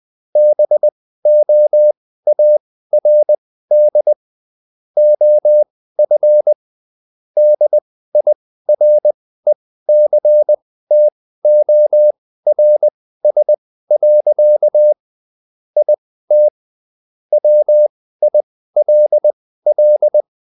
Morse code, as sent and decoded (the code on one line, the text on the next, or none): -... --- .- .-. -.. / --- ..-. / -.. .. .-. -.-. - --- .-. ... .-.-.- / .. - / .-- .. .-.. .-..
BOARD OF DIRECTORS. IT WILL